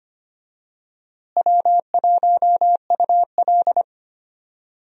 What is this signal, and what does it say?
Morse code, heard as a keyed tone, about 25 wpm, 700 Hz. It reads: W1UL
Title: USA Call Signs - 4 Characters - Repeated 3x - 25wpm